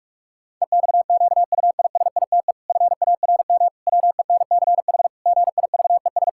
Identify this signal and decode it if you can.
Morse code, heard as a keyed tone, about 46 wpm, 700 Hz. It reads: EXQUISITE FARM WENCH GIVES
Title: Pangram Challenge v2 - 45wpm